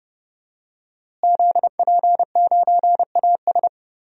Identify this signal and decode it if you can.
Morse code, heard as a keyed tone, about 32 wpm, 700 Hz. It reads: ZP9AS
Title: DXCC-100 Call Signs - 5 Characters - Repeated 3x - 30wpm